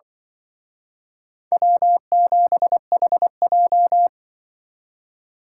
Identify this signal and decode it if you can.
Morse code, heard as a keyed tone, about 25 wpm, 700 Hz. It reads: W7HJ
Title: Speed Racing - USA Call Signs 20wpm